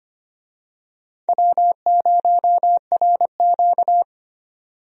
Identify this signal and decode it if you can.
Morse code, heard as a keyed tone, about 25 wpm, 700 Hz. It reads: W0RQ
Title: USA Call Signs - 4 Characters - 25wpm